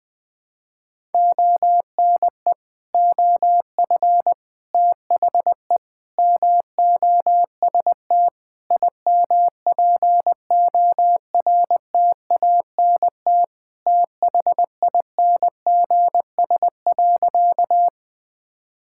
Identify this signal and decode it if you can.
Morse code, heard as a keyed tone, about 20 wpm, 700 Hz. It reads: ONE OF THE MOST IMPORTANT THINGS.
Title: Sentences with 4-Word Phrases - 20wpm